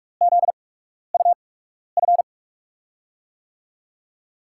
Morse code, heard as a keyed tone, about 44 wpm, spaced as for 12 wpm, 700 Hz. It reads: ZUF
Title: USA Call Signs - 6 Characters - 8x Character Spacing - 45wpm